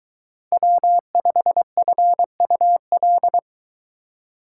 Morse code, heard as a keyed tone, about 23 wpm, 700 Hz. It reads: W5FUL